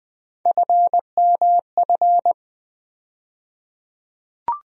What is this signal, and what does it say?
Morse code, heard as a keyed tone, about 20 wpm, 700 Hz. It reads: FMF
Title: USA Call Sign Postfixes - 3 Letters 20wpm